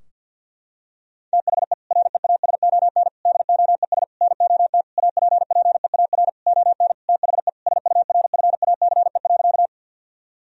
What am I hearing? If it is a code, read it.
Morse code, heard as a keyed tone, about 50 wpm, 700 Hz. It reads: THE REASON DOES NOT APPEAR ON THE SURFACE.
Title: Sentences from Top 1000 Words - Review Part 3 - 50wpm